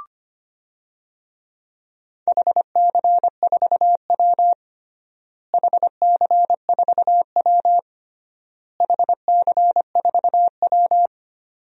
Morse code, heard as a keyed tone, about 25 wpm, 700 Hz. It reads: HC4W HC4W HC4W
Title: DXCC-100 Call Signs - 4 Characters - Repeated 3x - 25wpm